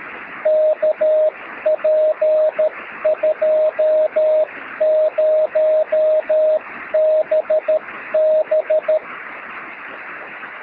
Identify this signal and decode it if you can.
Morse code, heard as a keyed tone, about 13 wpm, 620 Hz, with radio noise. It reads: KP20BB
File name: sm7lcb / oh2shf 13cm